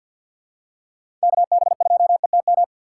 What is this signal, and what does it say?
Morse code, heard as a keyed tone, about 54 wpm, 700 Hz. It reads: KB2ETM
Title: USA Call Signs - 6 Characters - 8x Character Spacing - 50wpm